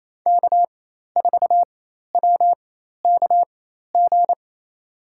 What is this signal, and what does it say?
Morse code, heard as a keyed tone, about 28 wpm, spaced as for 13 wpm, 700 Hz. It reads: K4WKG
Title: USA Call Signs - 5 Characters - 4x Character Spacing - 28wpm